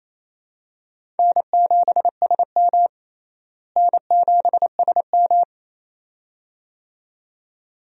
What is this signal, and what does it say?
Morse code, heard as a keyed tone, about 28 wpm, 700 Hz. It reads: N7SM N7SM
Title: USA Call Signs - 4 Characters - Repeated 2x - 28wpm